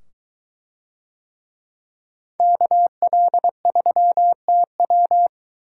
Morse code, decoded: KL3TW

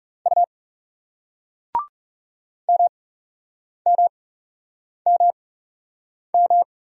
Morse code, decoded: U MMMM

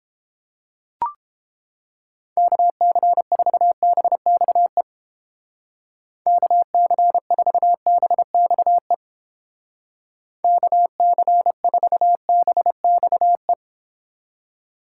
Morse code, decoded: KC4BXE KC4BXE KC4BXE

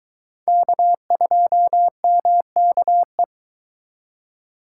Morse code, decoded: K2MKE

